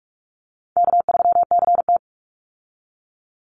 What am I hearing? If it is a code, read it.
Morse code, heard as a keyed tone, about 44 wpm, 700 Hz. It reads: K3CT
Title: Easy Call Signs Repeated 3x - 45wpm